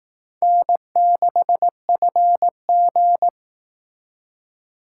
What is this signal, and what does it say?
Morse code, heard as a keyed tone, about 18 wpm, 700 Hz. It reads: N6FG
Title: Speed Racing - USA 1x2 Call Signs 15wpm